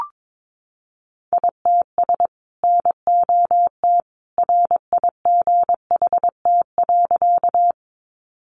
Morse code, heard as a keyed tone, about 22 wpm, 700 Hz. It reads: ITS NOT RIGHT.
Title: Sentences from Top 600 Words 22wpm